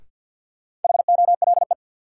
Morse code, heard as a keyed tone, about 52 wpm, 700 Hz. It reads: HOLE
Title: Sentences from Top 1600 Words - New Word Drill - 50wpm